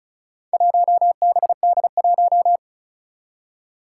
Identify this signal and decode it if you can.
Morse code, heard as a keyed tone, about 36 wpm, 700 Hz. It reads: 1BD1
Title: Four Challenging Letter-Number Combinations 35wpm